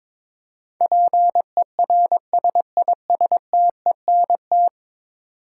Morse code, heard as a keyed tone, about 22 wpm, 700 Hz. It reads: PERSISTENT
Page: morse-code-ninja-practice-22wpm